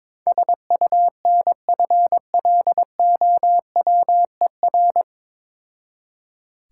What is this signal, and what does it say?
Morse code, heard as a keyed tone, about 22 wpm, 700 Hz. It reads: SUNFLOWER
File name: Difficult Words with Hints - 22wpm